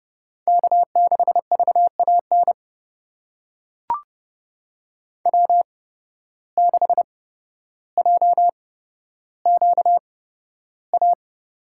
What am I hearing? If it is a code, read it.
Morse code, heard as a keyed tone, about 30 wpm, spaced as for 13 wpm, 700 Hz. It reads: K6VAN W6JQA
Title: USA Call Signs - 5 Characters - 8x Character Spacing - 30wpm